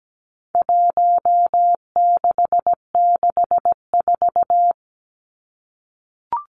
Morse code, heard as a keyed tone, about 17 wpm, 700 Hz. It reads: 1664
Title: Numbers - Contest Style - 4-Digit 17wpm